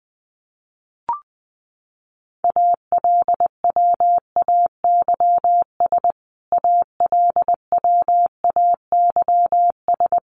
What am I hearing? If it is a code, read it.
Morse code, heard as a keyed tone, about 20 wpm, 700 Hz. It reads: ALWAYS ALWAYS
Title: Top 500 Words Repeated 20wpm